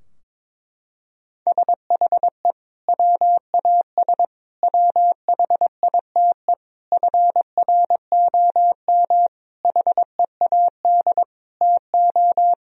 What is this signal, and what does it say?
Morse code, heard as a keyed tone, about 22 wpm, 700 Hz. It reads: SHE WAS WHITE FROM HEAD TO
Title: Sentences from Top 1000 Words - Review Part 3 - 22wpm